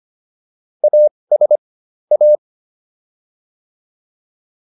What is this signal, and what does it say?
Morse code, heard as a keyed tone, about 25 wpm, spaced as for 20 wpm, 600 Hz. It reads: AS A